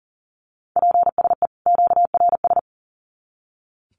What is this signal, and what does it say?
Morse code, heard as a keyed tone, about 40 wpm, 700 Hz. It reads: PSE QRS
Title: Common QSO Phrases 40wpm